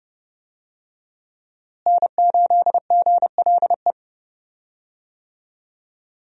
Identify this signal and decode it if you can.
Morse code, heard as a keyed tone, about 29 wpm, 700 Hz. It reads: N8GLE
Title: USA Call Signs - 5 Characters - Repeated 2x - 30wpm